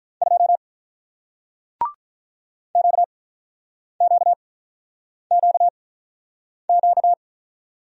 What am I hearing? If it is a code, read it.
Morse code, heard as a keyed tone, about 48 wpm, spaced as for 9 wpm, 700 Hz. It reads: 2 QQQQ